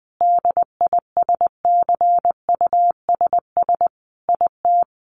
Morse code, heard as a keyed tone, about 20 wpm, 700 Hz. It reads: DISCUSS IT
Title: Sentences from Top 1100 Words 20wpm